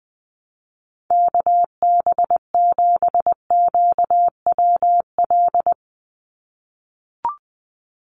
Morse code, heard as a keyed tone, about 20 wpm, 700 Hz. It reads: KB7QWL
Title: USA Call Signs 20wpm